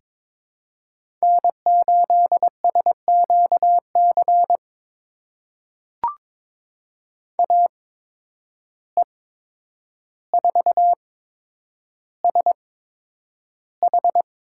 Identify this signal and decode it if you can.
Morse code, heard as a keyed tone, about 22 wpm, spaced as for 9 wpm, 700 Hz. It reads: N8SQC AE4SH
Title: USA Call Signs - 5 Characters - 8x Character Spacing - 22wpm